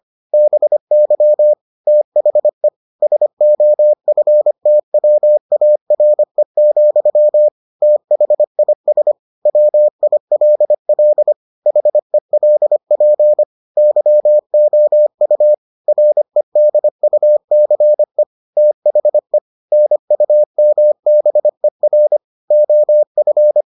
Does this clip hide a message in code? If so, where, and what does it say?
Yes: Morse code from 0.3 to 23.6 s, heard as a keyed tone, about 25 wpm, 600 Hz. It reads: BY THE SOFTWARE, THIS WILL HELP YOU REDUCE THE NUMBER OF